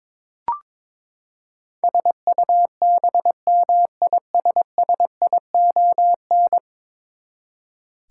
Morse code, heard as a keyed tone, about 22 wpm, 700 Hz. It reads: SUBMISSION
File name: practice / 22wpm / Ten Letter Words - 22wpm